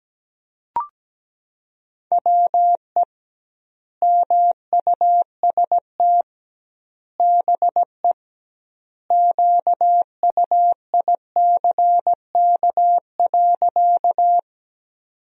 Morse code, heard as a keyed tone, about 17 wpm, 700 Hz. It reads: WE MUST BE QUICK.
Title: 2x Word-Spacing - Sentences from Top 500 Words - 17wpm